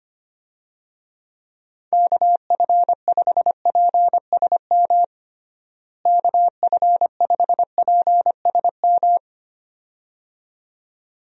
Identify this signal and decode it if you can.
Morse code, heard as a keyed tone, about 25 wpm, 700 Hz. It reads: KF5PSM KF5PSM
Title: USA Call Signs - 6 Characters - Repeated 2x - 25wpm